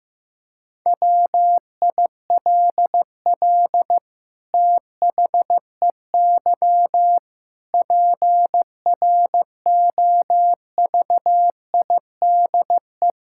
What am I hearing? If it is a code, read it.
Morse code, heard as a keyed tone, about 16 wpm, 700 Hz. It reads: WILL THEY PROVIDE
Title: Sentences from Top 1000 Words - Review Part 5 - 15wpm